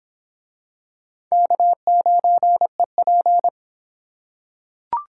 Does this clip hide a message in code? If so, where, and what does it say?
Yes: Morse code from 1.3 to 3.5 s, heard as a keyed tone, about 23 wpm, 700 Hz. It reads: K9EP